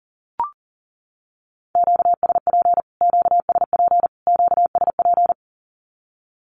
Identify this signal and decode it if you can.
Morse code, heard as a keyed tone, about 40 wpm, 700 Hz. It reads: QSP QSP QSP
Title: Q-Codes Repeated 3x 40wpm